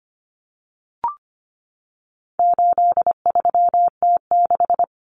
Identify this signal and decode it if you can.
Morse code, heard as a keyed tone, about 25 wpm, 700 Hz. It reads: 83T6
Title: Common Cut Numbers - 4-Digit 25wpm